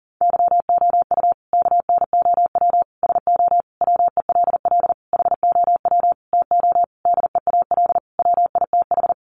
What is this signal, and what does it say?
Morse code, heard as a keyed tone, about 40 wpm, 700 Hz. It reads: YOU KNOW SO WELL HOW TO DEAL WITH
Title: Sentences from Top 800 Words 40wpm